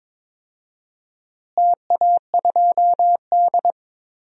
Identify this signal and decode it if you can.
Morse code, heard as a keyed tone, about 22 wpm, 700 Hz. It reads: TA2D
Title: DXCC-100 Call Signs - 4 Characters - Repeated 2x - 22wpm